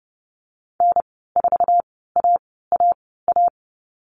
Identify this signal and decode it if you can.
Morse code, heard as a keyed tone, about 30 wpm, spaced as for 17 wpm, 700 Hz. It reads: N4AAA
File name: Lesson 14g - Number 2 - Call Signs 30wpm